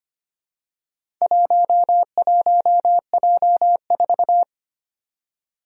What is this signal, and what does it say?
Morse code, heard as a keyed tone, about 25 wpm, 700 Hz. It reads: 11J4
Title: Four Challenging Letter-Number Combinations 25wpm